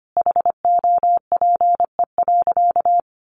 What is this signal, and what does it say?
Morse code, heard as a keyed tone, about 25 wpm, 700 Hz. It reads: HOPE.
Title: Sentences from Top 1000 Words 25wpm